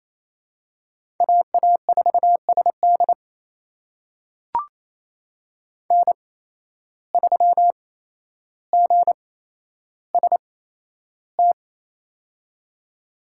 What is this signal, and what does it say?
Morse code, heard as a keyed tone, about 28 wpm, spaced as for 12 wpm, 700 Hz. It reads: AA4SD N3GST
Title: USA Call Signs - 5 Characters - 8x Character Spacing - 28wpm